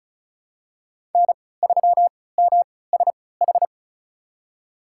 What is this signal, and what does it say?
Morse code, heard as a keyed tone, about 34 wpm, spaced as for 20 wpm, 700 Hz. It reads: N3MSH
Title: USA Call Signs - 3x Character Spacing - 35wpm